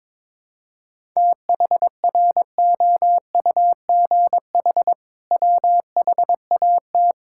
Morse code, heard as a keyed tone, about 22 wpm, 700 Hz. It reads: THROUGH WHAT
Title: Sentences from Top 1000 Words - Review Part 6 - 22wpm